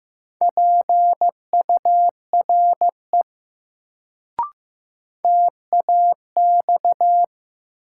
Morse code, heard as a keyed tone, about 15 wpm, 700 Hz. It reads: PURE TAX
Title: Sentences from Top 1500 Words - New Word Drill Alt - 15wpm